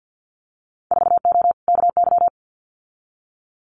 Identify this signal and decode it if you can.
Morse code, heard as a keyed tone, about 50 wpm, 700 Hz. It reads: 4O KY
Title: Winter Field Day Exchanges Encore 50wpm